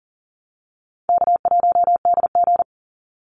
Morse code, heard as a keyed tone, about 40 wpm, 700 Hz. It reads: K1DG
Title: Easy Call Signs Repeated 2x - 40wpm